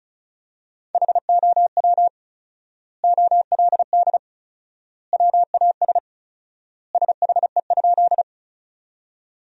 Morse code, heard as a keyed tone, about 36 wpm, 700 Hz. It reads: HOW OLD WAS SHE?
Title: 4x Word-Spacing - Sentences from Top 200 Words - 35wpm